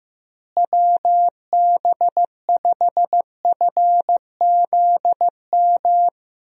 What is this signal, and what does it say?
Morse code, heard as a keyed tone, about 15 wpm, 700 Hz. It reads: WB5FZM
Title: USA Call Signs - 6 Characters - Repeated 3x - 15wpm